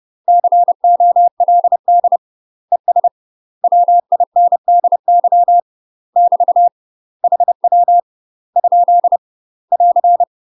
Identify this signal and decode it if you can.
Morse code, heard as a keyed tone, about 30 wpm, 700 Hz. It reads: COLD ES WINDY <BT> HW ? <AR>
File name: UK MASTER QSOs TRAINING / 30WPM / UK MASTER QSOs 0049